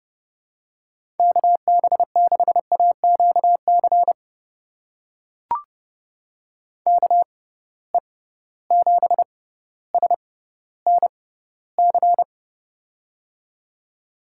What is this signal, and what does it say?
Morse code, heard as a keyed tone, about 30 wpm, spaced as for 15 wpm, 700 Hz. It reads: KB6AQC KE7SNC
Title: USA Call Signs - 6 Characters - 6x Character Spacing - 30wpm